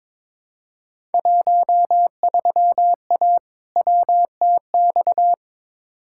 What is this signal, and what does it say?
Morse code, heard as a keyed tone, about 21 wpm, 700 Hz. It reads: 13A WTX
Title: Field Day Exchanges Encore - 22wpm